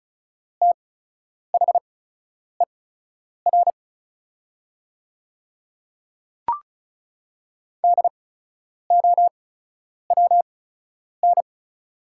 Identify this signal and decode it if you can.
Morse code, heard as a keyed tone, about 36 wpm, spaced as for 10 wpm, 690 Hz. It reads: THER DOWN
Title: 8x Character Spacing - Top 500 Words - 35wpm